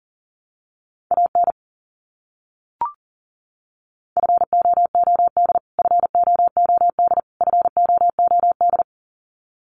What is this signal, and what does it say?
Morse code, heard as a keyed tone, about 40 wpm, 700 Hz. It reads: AN FOOD FOOD FOOD
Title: Top 300 Words Repeated 40wpm